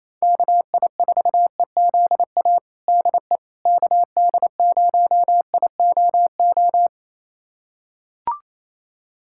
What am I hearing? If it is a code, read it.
Morse code, heard as a keyed tone, about 28 wpm, 700 Hz. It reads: KI4EZA DE KD0IOO
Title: USA Callsign DE Callsign - 6 Characters - 28wpm